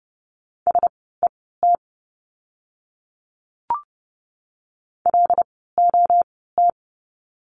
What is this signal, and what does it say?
Morse code, heard as a keyed tone, about 30 wpm, spaced as for 17 wpm, 700 Hz. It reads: SET LOT
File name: Lesson 16f - Letter c - Words 30wpm